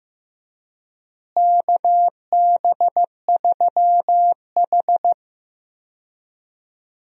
Morse code, heard as a keyed tone, about 16 wpm, 700 Hz. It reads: KB3H
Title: USA Call Signs - 4 Characters - Repeated 3x - 15wpm